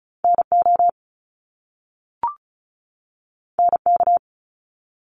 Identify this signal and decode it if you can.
Morse code, heard as a keyed tone, about 36 wpm, 700 Hz. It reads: N O N K
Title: Top Two Letter Combinations in Words 35wpm